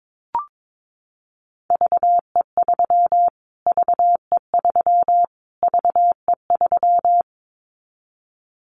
Morse code, heard as a keyed tone, about 22 wpm, 700 Hz. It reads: VE3 VE3 VE3
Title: Callsign Prefixes for Top 25 DXCC - Repeated 3x - 22wpm